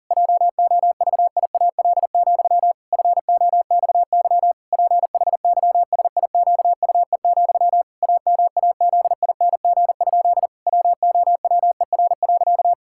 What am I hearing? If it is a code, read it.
Morse code, heard as a keyed tone, about 40 wpm, 700 Hz. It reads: JOVIAL, FOXY PHYSIQUE, AMAZING? WOWER.
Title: Pangram Challenge v2 - 40wpm